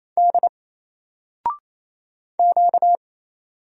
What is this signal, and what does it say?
Morse code, heard as a keyed tone, about 27 wpm, 700 Hz. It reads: D Q
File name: Single-Letter-Number-28wpm